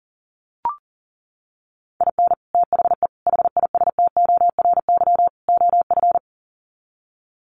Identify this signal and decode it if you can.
Morse code, heard as a keyed tone, about 40 wpm, 700 Hz. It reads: IN THE HISTORY OF
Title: Sets of 4 Words 40wpm